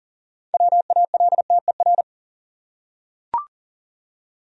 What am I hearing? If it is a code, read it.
Morse code, heard as a keyed tone, about 40 wpm, 700 Hz. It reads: WALTER
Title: USA Common Names - 40wpm